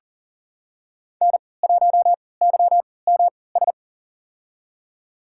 Morse code, heard as a keyed tone, about 40 wpm, spaced as for 23 wpm, 700 Hz. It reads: N1YMS
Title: USA Call Signs - 5 Characters - 3x Character Spacing - 40wpm